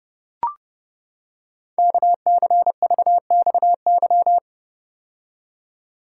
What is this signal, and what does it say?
Morse code, heard as a keyed tone, about 30 wpm, 700 Hz. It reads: KCVXY
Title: Code Groups - 5 Letters - 30wpm